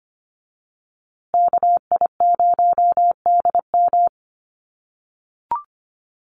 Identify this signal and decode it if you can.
Morse code, heard as a keyed tone, about 25 wpm, 700 Hz. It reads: KI0DM
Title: USA Call Signs Repeated 3x 25wpm